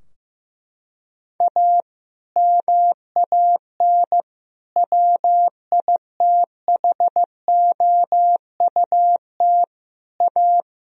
Morse code, heard as a keyed tone, about 15 wpm, 700 Hz. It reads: A MAN WITHOUT A
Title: Sentences from Top 1000 Words - Review Part 3 - 15wpm